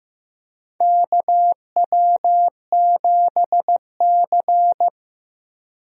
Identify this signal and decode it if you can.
Morse code, heard as a keyed tone, about 15 wpm, 700 Hz. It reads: KW7C
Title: USA Call Signs - 4 Characters - Repeated 2x - 15wpm